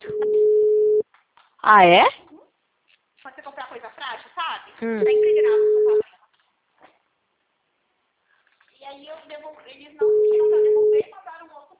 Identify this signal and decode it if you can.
Ringback tone (Europe, 425 Hz), with speech.
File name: out- -2003- - - .6060